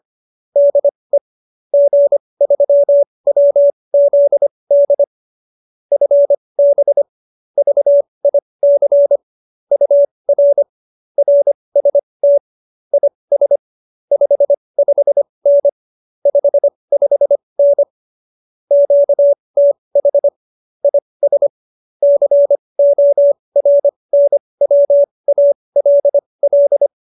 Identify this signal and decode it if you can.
Morse code, heard as a keyed tone, about 25 wpm, spaced as for 20 wpm, 570 Hz. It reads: DE G3WZD FB VIC UR RST IS 55N 55N QTH IS CORNWALL
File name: QSO310 20